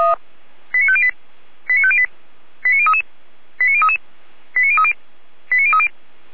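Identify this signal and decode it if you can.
Five-tone selective call (ZVEI), heard as quick runs of five tones, tones of about 70 ms. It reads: 89508 89508 89033 89033 89030 89030
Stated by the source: ZVEI five-tone call received by SDR, Feuerwehr Orlamünde (f85360).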